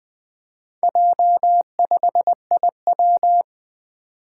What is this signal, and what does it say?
Morse code, heard as a keyed tone, about 20 wpm, 700 Hz. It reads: J5IW
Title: Four Letter-Number Combinations - 20wpm